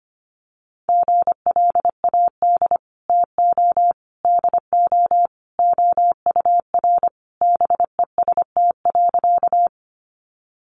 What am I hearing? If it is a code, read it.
Morse code, heard as a keyed tone, about 25 wpm, 700 Hz. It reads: GLAD TO DO OUR BEST.
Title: Sentences from Top 800 Words 25wpm